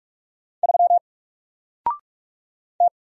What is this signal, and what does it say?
Morse code, heard as a keyed tone, about 44 wpm, 700 Hz. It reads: 3 T
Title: Single Letter-Number 45wpm